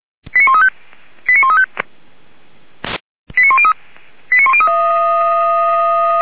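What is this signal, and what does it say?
Five-tone selective call (ZVEI), heard as quick runs of five tones, tones of about 70 ms. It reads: 89135 89135 89103 89103
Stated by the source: ZVEI five-tone selective call received by SDR, from fire brigade Feuerwehr Orlamünde.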